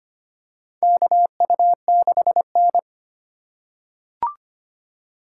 Morse code, heard as a keyed tone, about 25 wpm, 700 Hz. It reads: KU6N
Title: USA Call Signs - 4 Characters - 3x Character Spacing - 25wpm